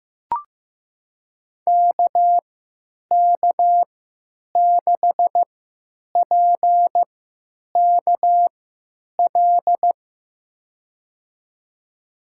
Morse code, heard as a keyed tone, about 15 wpm, spaced as for 8 wpm, 700 Hz. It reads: KK6PKL
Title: USA Call Signs - 3x Character Spacing - 15wpm